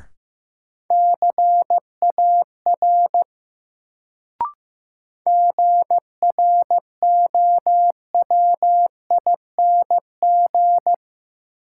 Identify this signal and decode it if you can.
Morse code, heard as a keyed tone, about 15 wpm, 700 Hz. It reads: CAR GROWING
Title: Sets of 5 Words Encore - New Word Drill - 15wpm